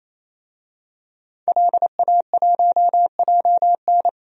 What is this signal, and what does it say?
Morse code, heard as a keyed tone, about 27 wpm, 700 Hz. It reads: LA1JN
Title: DXCC-100 Call Signs - 5 Characters - 28wpm